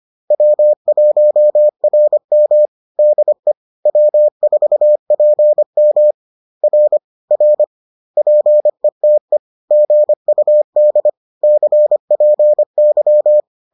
Morse code, heard as a keyed tone, about 25 wpm, 600 Hz. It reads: W1RM DE W4PM R R PETE GUD CPY